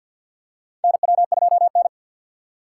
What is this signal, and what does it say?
Morse code, heard as a keyed tone, about 52 wpm, 700 Hz. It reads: NW2N
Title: USA Call Signs - 3x Character Spacing - 50wpm